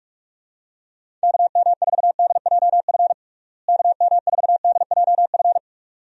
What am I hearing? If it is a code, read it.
Morse code, heard as a keyed tone, about 46 wpm, 700 Hz. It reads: KM4DJF KM4DJF
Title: USA Call Signs - 6 Characters - Repeated 2x - 45wpm